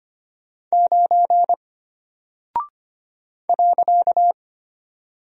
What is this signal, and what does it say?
Morse code, heard as a keyed tone, about 25 wpm, 700 Hz. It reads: 9 .
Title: Single Character (Letter:Number:Punctuation:Prosign) Alt 25wpm